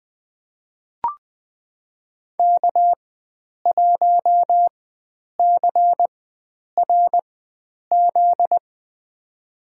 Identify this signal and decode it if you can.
Morse code, heard as a keyed tone, about 20 wpm, spaced as for 9 wpm, 700 Hz. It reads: K1CRZ
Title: USA Call Signs - 5 Characters - 4x Character Spacing - 20wpm